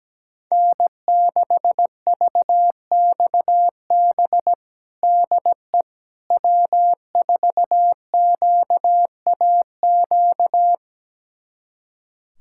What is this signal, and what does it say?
Morse code, heard as a keyed tone, about 17 wpm, 700 Hz. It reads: N6VXB DE W4QAQ
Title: USA Callsign DE Callsign - 5 Characters - 17wpm